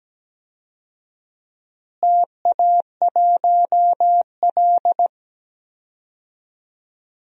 Morse code, heard as a keyed tone, about 17 wpm, 700 Hz. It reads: TA1L
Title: DXCC-100 Call Signs - 4 Characters - Repeated 2x - 17wpm